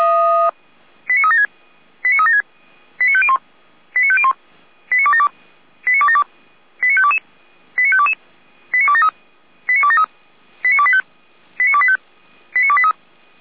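Five-tone selective call (ZVEI), heard as quick runs of five tones, tones of about 70 ms. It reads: 89376 89376 89501 89501 89272 89272 89522 89522 89273 89273 89285 89285 89283
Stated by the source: ZVEI five-tone selective call received by SDR.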